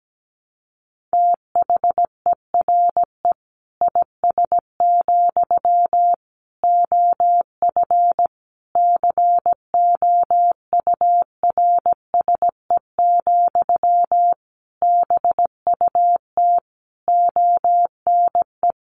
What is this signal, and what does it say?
Morse code, heard as a keyed tone, about 17 wpm, 700 Hz. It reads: THERE IS, OF COURSE, BUT ONE
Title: Sentences from Top 1200 Words 17wpm